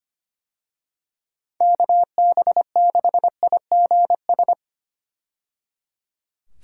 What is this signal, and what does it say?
Morse code, heard as a keyed tone, about 25 wpm, 700 Hz. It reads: KB6IGS
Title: USA Call Signs - 6 Characters - 25wpm